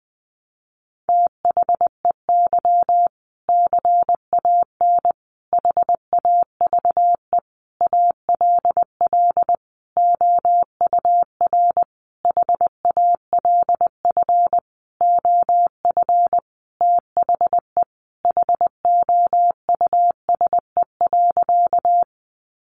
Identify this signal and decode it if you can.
Morse code, heard as a keyed tone, about 20 wpm, 700 Hz. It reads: THEY CAN HAVE ALL OUR HALF OF THE HOUSE.
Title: Sentences from Top 400 Words 20wpm